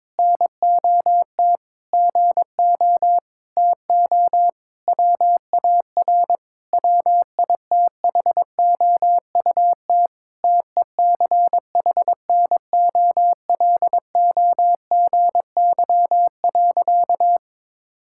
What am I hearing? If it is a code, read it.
Morse code, heard as a keyed tone, about 22 wpm, 700 Hz. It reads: NOT GO TO WAR WITHOUT TECHNOLOGY.
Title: Sentences from Top 1000 Words - Review Part 5 - 22wpm